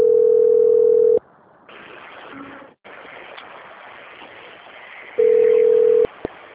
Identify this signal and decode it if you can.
Ringback tone (North America), 440 + 480 Hz.